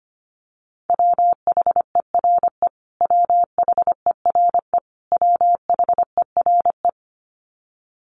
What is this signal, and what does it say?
Morse code, heard as a keyed tone, about 25 wpm, 700 Hz. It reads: WHERE WHERE WHERE
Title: Top 500 Words Repeated 25wpm